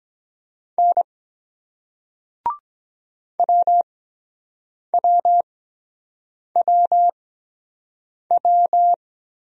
Morse code, heard as a keyed tone, about 23 wpm, spaced as for 7 wpm, 700 Hz. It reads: N WWWW